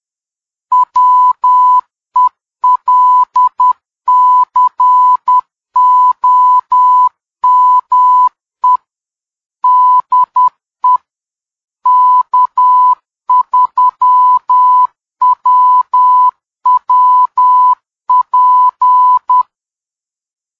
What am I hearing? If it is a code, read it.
Morse code, heard as a keyed tone, about 10 wpm, 1000 Hz. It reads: WELCOME DE K3WWP